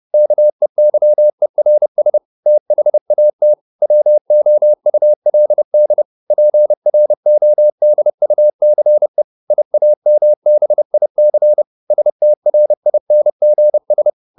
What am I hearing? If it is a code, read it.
Morse code, heard as a keyed tone, about 30 wpm, 600 Hz. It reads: KEYERS THAT WOULD PRODUCE IAMBIC STRINGS